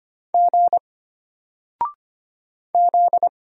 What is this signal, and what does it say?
Morse code, heard as a keyed tone, about 25 wpm, 700 Hz. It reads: G Z
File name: Single-Letter-Number-25wpm